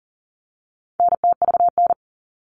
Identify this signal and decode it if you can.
Morse code, heard as a keyed tone, about 40 wpm, 700 Hz. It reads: NTVN
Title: All Cut Numbers - 4-Digit 40wpm